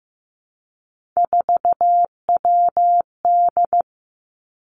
Morse code, heard as a keyed tone, about 15 wpm, 700 Hz. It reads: 4WD
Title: Three Challenging Letter-Number Combinations 15wpm